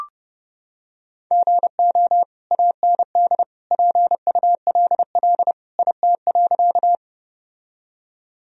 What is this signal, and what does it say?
Morse code, heard as a keyed tone, about 31 wpm, 700 Hz. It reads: GO AND PULL IT.
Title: Sentences from Top 1000 Words - Review Part 2 - 30wpm